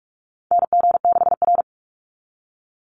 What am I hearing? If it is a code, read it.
Morse code, heard as a keyed tone, about 44 wpm, 700 Hz. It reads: NG6R
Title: NA SOTA Chaser Callsigns 45wpm